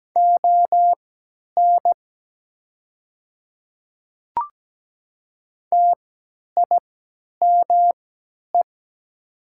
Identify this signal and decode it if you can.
Morse code, heard as a keyed tone, about 17 wpm, spaced as for 10 wpm, 700 Hz. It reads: ON TIME
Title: 3x Character Spacing - Top 100 Words - 17wpm